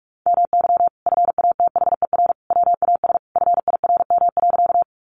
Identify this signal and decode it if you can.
Morse code, heard as a keyed tone, about 44 wpm, 700 Hz. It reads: MY FATHER WAS FIRM.
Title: Sentences from Top 1400 Words 45wpm